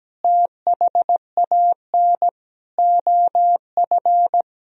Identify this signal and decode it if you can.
Morse code, heard as a keyed tone, about 17 wpm, 700 Hz. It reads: THAN OF
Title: Sentences from Top 1000 Words - Review Part 3 - 17wpm